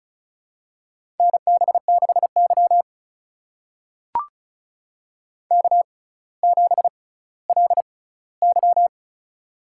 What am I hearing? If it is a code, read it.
Morse code, heard as a keyed tone, about 36 wpm, spaced as for 18 wpm, 700 Hz. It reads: NB6Y K7LY
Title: USA Call Signs - 4 Characters - 6x Character Spacing - 35wpm